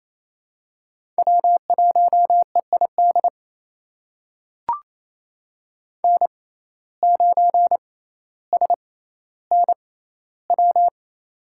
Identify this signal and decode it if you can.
Morse code, heard as a keyed tone, about 28 wpm, spaced as for 14 wpm, 700 Hz. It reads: W1EID N9SNW
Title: USA Call Signs - 5 Characters - 6x Character Spacing - 28wpm